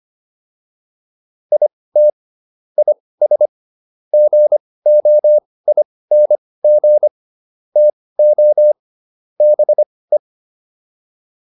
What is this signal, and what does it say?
Morse code, heard as a keyed tone, about 25 wpm, spaced as for 18 wpm, 600 Hz. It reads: IT IS GOING TO BE